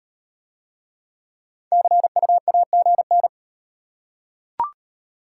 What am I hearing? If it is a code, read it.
Morse code, heard as a keyed tone, about 38 wpm, 690 Hz. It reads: CU AGN